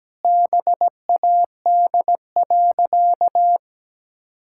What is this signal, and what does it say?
Morse code, heard as a keyed tone, about 17 wpm, 700 Hz. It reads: BAD.
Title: Sentences from Top 1600 Words 17wpm